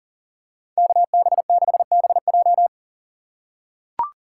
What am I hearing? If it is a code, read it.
Morse code, heard as a keyed tone, about 40 wpm, 700 Hz. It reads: KB6BJ
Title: USA Call Signs - 5 Characters - 4x Character Spacing - 40wpm